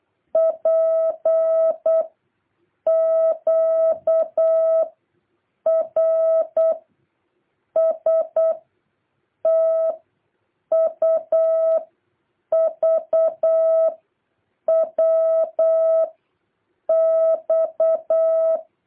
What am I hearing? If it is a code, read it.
Morse code, heard as a keyed tone, about 8 wpm, 650 Hz. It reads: P Q R S T U V W X
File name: jh8cbh / 2023cw